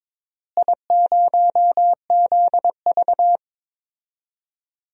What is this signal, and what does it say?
Morse code, heard as a keyed tone, about 22 wpm, 700 Hz. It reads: I0ZV
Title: DXCC-100 Call Signs - 4 Characters - Repeated 2x - 22wpm